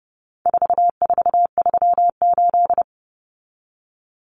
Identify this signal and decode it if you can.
Morse code, heard as a keyed tone, about 30 wpm, 700 Hz. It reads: 4438
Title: Numbers - 4-Digit 30wpm